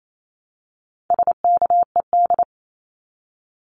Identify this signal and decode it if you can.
Morse code, heard as a keyed tone, about 28 wpm, 700 Hz. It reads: SKED
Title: Abbreviations Repeated 3x - 28wpm